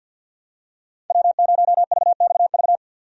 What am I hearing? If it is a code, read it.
Morse code, heard as a keyed tone, about 48 wpm, 700 Hz. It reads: W0VXV
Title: USA Call Signs - 5 Characters - 50wpm